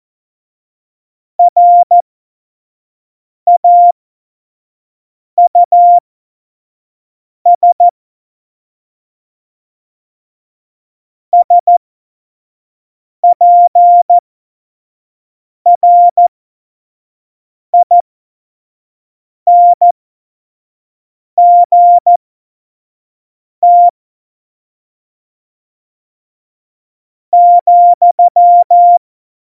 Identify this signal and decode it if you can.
Morse code, heard as a keyed tone, about 14 wpm, spaced as for 5 wpm, 700 Hz. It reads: RAUS SPRINGT,
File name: Morsekurs / Witze / Witz 11. Dezember 2018, 25 F70